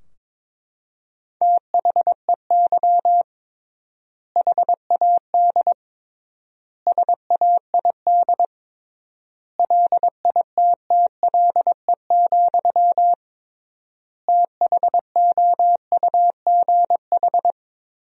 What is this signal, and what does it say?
Morse code, heard as a keyed tone, about 22 wpm, 700 Hz. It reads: THEY HAD SAID LITTLE, THOUGH
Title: 3x Word-Spacing - Sentences from Top 400 Words - 22wpm